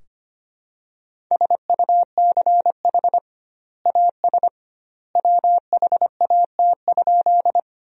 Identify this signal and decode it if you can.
Morse code, heard as a keyed tone, about 25 wpm, 700 Hz. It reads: SUCH AS WHAT?
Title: 2x Word-Spacing - Sentences from Top 200 Words - 25wpm